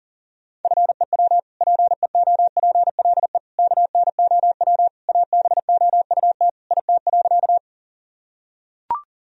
Morse code, heard as a keyed tone, about 37 wpm, 700 Hz. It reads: FEW PEOPLE KNOW ABOUT IT.
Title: Sentences from Top 1000 Words - Review Part 1 - 40wpm